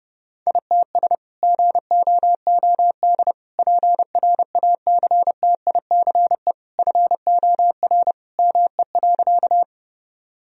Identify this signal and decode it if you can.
Morse code, heard as a keyed tone, about 32 wpm, 700 Hz. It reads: ITS GOOD PRACTICE FOR ME.
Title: Sentences from Top 1000 Words - Review Part 4 - 30wpm